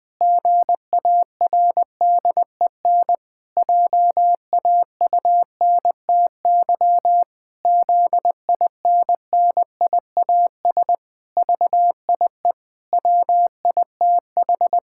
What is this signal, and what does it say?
Morse code, heard as a keyed tone, about 20 wpm, 700 Hz. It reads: GARDEN JAUNTY ZINNIAS VIE WITH
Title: Pangram Challenge v2 - 20wpm